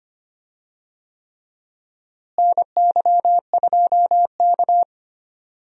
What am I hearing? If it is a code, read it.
Morse code, heard as a keyed tone, about 25 wpm, 700 Hz. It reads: NY2K
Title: USA Call Signs - 4 Characters - Repeated 3x - 25wpm